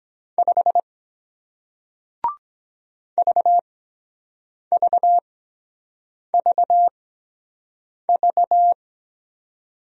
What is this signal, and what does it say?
Morse code, heard as a keyed tone, about 23 wpm, spaced as for 7 wpm, 700 Hz. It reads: 5 VVVV